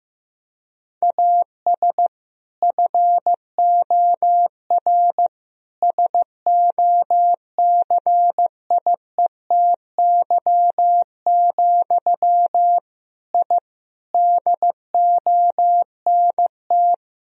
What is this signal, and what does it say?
Morse code, heard as a keyed tone, about 15 wpm, 700 Hz. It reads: AS FOR SOCIETY, I DONT
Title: Sentences from Top 1000 Words - Review Part 4 - 15wpm